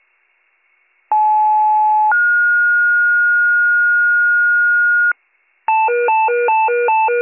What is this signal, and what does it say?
Two-tone page, heard as a one-second tone, then a three-second tone, about 860 then 1430 Hz, with noise. It is heard from 1.1 to 5.1 s.